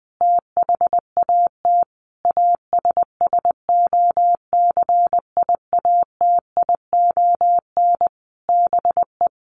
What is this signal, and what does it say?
Morse code, heard as a keyed tone, about 20 wpm, 700 Hz. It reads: THAT ASSOCIATION BE
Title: Sentences from Top 1300 Words 20wpm